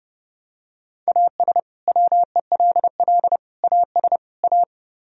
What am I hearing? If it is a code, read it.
Morse code, heard as a keyed tone, about 32 wpm, 700 Hz. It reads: AS WELL AS A